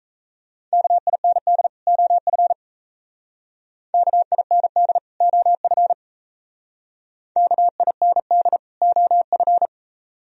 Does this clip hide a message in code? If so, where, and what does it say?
Yes: Morse code from 0.7 to 9.7 s, heard as a keyed tone, about 36 wpm, 690 Hz. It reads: KIND OF KIND OF KIND OF